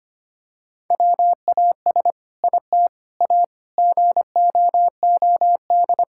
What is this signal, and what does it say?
Morse code, heard as a keyed tone, about 25 wpm, 700 Hz. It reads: WAS IT A GOOD
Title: Sentences from Top 1000 Words - Review Part 5 - 25wpm